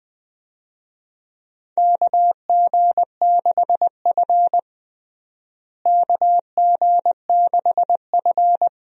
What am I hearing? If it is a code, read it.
Morse code, heard as a keyed tone, about 21 wpm, 700 Hz. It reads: KG6F KG6F
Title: USA Call Signs - 4 Characters - Repeated 2x - 20wpm